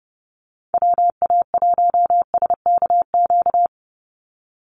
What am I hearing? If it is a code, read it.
Morse code, heard as a keyed tone, about 30 wpm, 700 Hz. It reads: WA1SKQ
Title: Call Signs - 30wpm